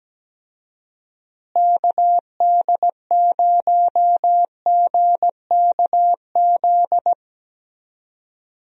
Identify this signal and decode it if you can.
Morse code, heard as a keyed tone, about 17 wpm, 700 Hz. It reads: KD0GKZ
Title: USA Call Signs - 6 Characters - Repeated 3x - 17wpm